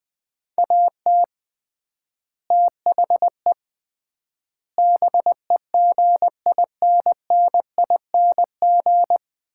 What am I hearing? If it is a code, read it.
Morse code, heard as a keyed tone, about 20 wpm, 700 Hz. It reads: AT THE BEGINNING
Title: 3x Word-Spacing - Sets of 5 Words - 20wpm